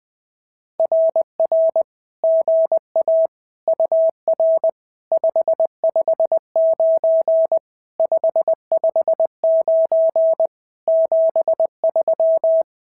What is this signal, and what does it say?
Morse code, heard as a keyed tone, about 19 wpm, 650 Hz. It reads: RR GA UR 559 559 73
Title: SOTA QSO Practice 20wpm